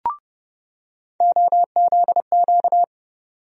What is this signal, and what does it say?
Morse code, heard as a keyed tone, about 30 wpm, 700 Hz. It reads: OZQ
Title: USA Call Sign Postfixes - 3 Letters 30wpm